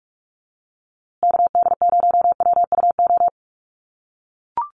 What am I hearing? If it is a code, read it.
Morse code, heard as a keyed tone, about 44 wpm, 700 Hz. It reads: KD0WUO
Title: USA Call Signs Repeated 3x 45wpm